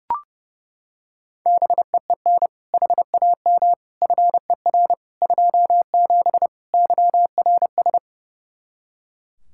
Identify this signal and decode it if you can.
Morse code, heard as a keyed tone, about 30 wpm, 700 Hz. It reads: BEEN HAM FER 27 YRS
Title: Practice QSO - Exchange 3 - More Info - 30wpm